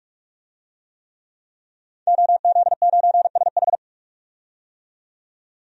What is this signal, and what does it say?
Morse code, heard as a keyed tone, about 43 wpm, 700 Hz. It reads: OZ9SH
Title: DXCC-100 Call Signs - 5 Characters - 45wpm